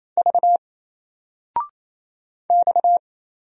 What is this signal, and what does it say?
Morse code, heard as a keyed tone, about 27 wpm, 700 Hz. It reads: V X
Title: Single Letter-Number 28wpm